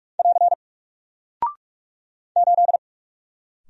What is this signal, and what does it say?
Morse code, heard as a keyed tone, about 44 wpm, 700 Hz. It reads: <AR> 8
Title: Single Character (Letter:Number:Punctuation:Prosign) Alt 45wpm